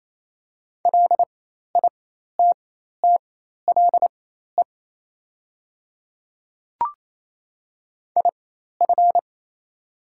Morse code, heard as a keyed tone, about 27 wpm, spaced as for 13 wpm, 690 Hz. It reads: LITTLE IF
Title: 4x Character Spacing - Top 200 Words - 28wpm